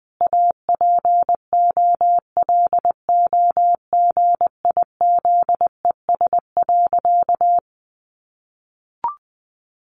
Morse code, heard as a keyed tone, about 20 wpm, 700 Hz. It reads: APOLOGIZES.
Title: Additional Easy Phrases 20wpm